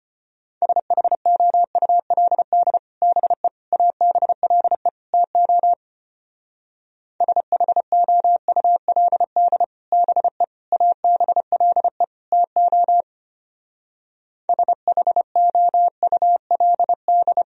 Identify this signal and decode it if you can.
Morse code, heard as a keyed tone, about 31 wpm, 700 Hz. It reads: SHOULD BE ABLE TO SHOULD BE ABLE TO SHOULD